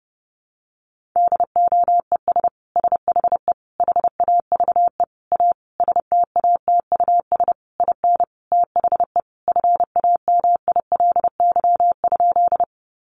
Morse code, heard as a keyed tone, about 30 wpm, 700 Hz. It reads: DOES SHE HAVE A STATUS IN THE FAMILY?
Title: Sentences from Top 1400 Words 30wpm